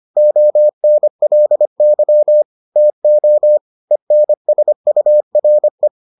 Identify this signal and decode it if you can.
Morse code, heard as a keyed tone, about 25 wpm, 600 Hz. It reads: ONLY TO ENSURE